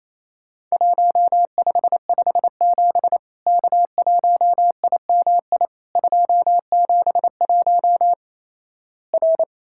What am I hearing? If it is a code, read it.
Morse code, heard as a keyed tone, about 28 wpm, 700 Hz. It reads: 1557 K1IMI 271 R
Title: Worked All Europe DX Contest - QTC Exchange - 28wpm